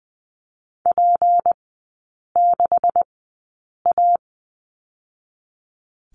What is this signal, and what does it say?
Morse code, heard as a keyed tone, about 20 wpm, spaced as for 8 wpm, 700 Hz. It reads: P6A
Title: 152 Lesson 25e - Letter p - Three Characters 20wpm